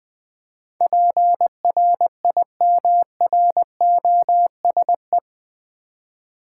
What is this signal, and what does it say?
Morse code, heard as a keyed tone, about 20 wpm, 700 Hz. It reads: PRIMROSE
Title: Difficult Words with Hints - 20wpm